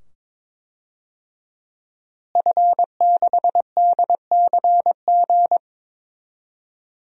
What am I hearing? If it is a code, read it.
Morse code, heard as a keyed tone, about 22 wpm, 700 Hz. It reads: F6DCG
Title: DXCC-100 Call Signs - 5 Characters - Repeated 2x - 22wpm